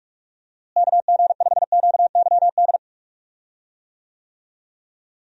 Morse code, heard as a keyed tone, about 48 wpm, 700 Hz. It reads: KG5QYD
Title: USA Call Signs - 6 Characters - Repeated 3x - 45wpm